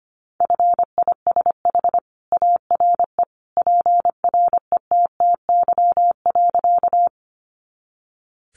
Morse code, heard as a keyed tone, about 25 wpm, 700 Hz. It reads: FISH ARE PRETTY.
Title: Additional Easy Phrases 25wpm